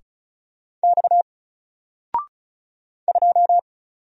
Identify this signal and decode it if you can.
Morse code, heard as a keyed tone, about 36 wpm, 700 Hz. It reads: X 2
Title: Single Letter-Number 35wpm